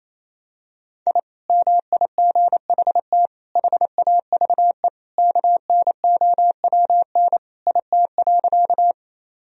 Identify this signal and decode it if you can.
Morse code, heard as a keyed tone, about 28 wpm, 700 Hz. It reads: I MIGHT HAVE KNOWN IT.
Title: Sentences from Top 1000 Words - Review Part 3 - 28wpm